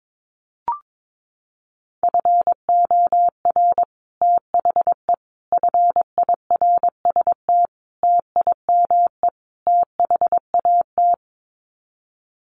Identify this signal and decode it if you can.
Morse code, heard as a keyed tone, about 22 wpm, 700 Hz. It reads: FOR THE FIRST TIME THAT
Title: Sets of 5 Words Encore 22wpm